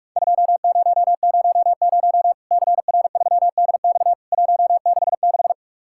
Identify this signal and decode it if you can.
Morse code, heard as a keyed tone, about 43 wpm, 700 Hz. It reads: 2000 CR3DX 166
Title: Worked All Europe DX Contest - QTC Exchange - 45wpm